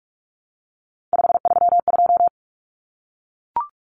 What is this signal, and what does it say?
Morse code, heard as a keyed tone, about 44 wpm, 700 Hz. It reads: 532
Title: Common Cut Numbers - 3-Digit 45wpm